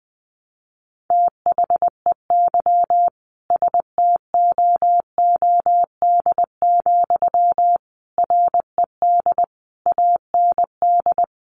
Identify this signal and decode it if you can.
Morse code, heard as a keyed tone, about 20 wpm, 700 Hz. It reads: THEY STOOD, RED AND
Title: A Sound Of Thunder 20wpm